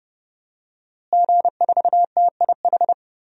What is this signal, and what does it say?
Morse code, heard as a keyed tone, about 32 wpm, 700 Hz. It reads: G4TIH
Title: DXCC-100 Call Signs - 5 Characters - Repeated 2x - 30wpm